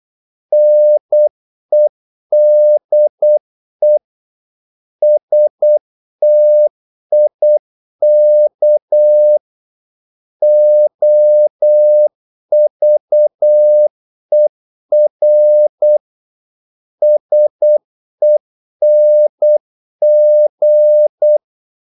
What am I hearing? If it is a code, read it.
Morse code, heard as a keyed tone, about 8 wpm, 600 Hz. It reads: NEDE STIK OVER SENG
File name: Korte ord DK 8wpm